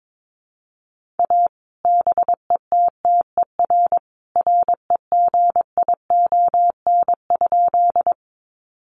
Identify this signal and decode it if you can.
Morse code, heard as a keyed tone, about 22 wpm, 700 Hz. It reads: A BETTER REGION?
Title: Sentences from Top 1500 Words 22wpm